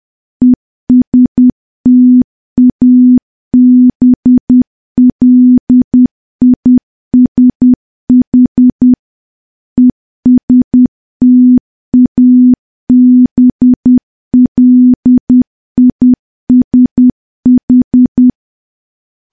Morse code, heard as a keyed tone, about 10 wpm, 250 Hz. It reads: ESTABLISH ESTABLISH
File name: morse / morse-training-10WPM